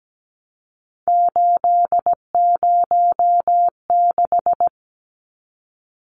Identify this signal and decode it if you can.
Morse code, heard as a keyed tone, about 17 wpm, 700 Hz. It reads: 806
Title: Numbers - 3-Digit 17wpm